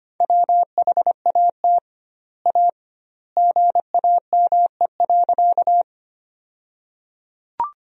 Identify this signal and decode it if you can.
Morse code, heard as a keyed tone, about 25 wpm, 700 Hz. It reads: WHAT A GAME.
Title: 2x Word-Spacing - Sentences from Top 500 Words - 25wpm